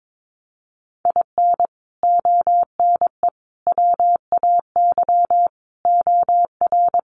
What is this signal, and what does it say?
Morse code, heard as a keyed tone, about 22 wpm, 700 Hz. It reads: IN ONE WAY OR
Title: Sets of 5 Words Encore 22wpm